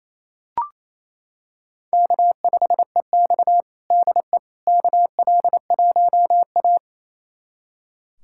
Morse code, heard as a keyed tone, about 28 wpm, 700 Hz. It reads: K5EX DE KL1A
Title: USA Callsign DE Callsign - 4 Characters - 28wpm